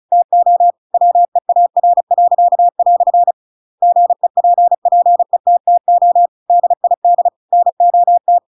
Morse code, heard as a keyed tone, about 34 wpm, 700 Hz. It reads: TO WEAR." GEPPETTO DID NOT
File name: Pinocchio / Pinocchio ch8 35 35 ik1hge